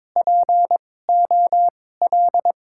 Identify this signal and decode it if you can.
Morse code, heard as a keyed tone, about 22 wpm, spaced as for 16 wpm, 700 Hz. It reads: POL
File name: Top 90 QSO Elements - 2x Character Spacing - 22wpm